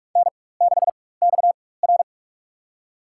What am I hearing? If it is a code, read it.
Morse code, heard as a keyed tone, about 46 wpm, spaced as for 21 wpm, 700 Hz. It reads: N6XR
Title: USA Call Signs - 4 Characters - 4x Character Spacing - 45wpm